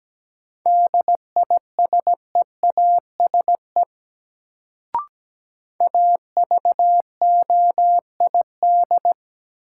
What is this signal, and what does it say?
Morse code, heard as a keyed tone, about 17 wpm, 700 Hz. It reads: DISEASE AVOID
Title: Sentences from Top 1000 Words - Review Part 6 - New Word Drill - 17wpm